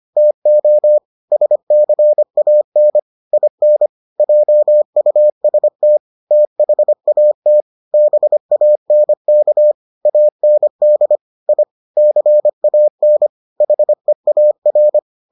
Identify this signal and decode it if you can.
Morse code, heard as a keyed tone, about 25 wpm, 600 Hz. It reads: TO SCAN IN JUST THAT BANK AND I CAN HEAR